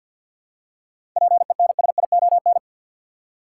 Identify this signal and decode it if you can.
Morse code, heard as a keyed tone, about 51 wpm, 700 Hz. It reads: PENSION
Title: Seven Letter Words - 50wpm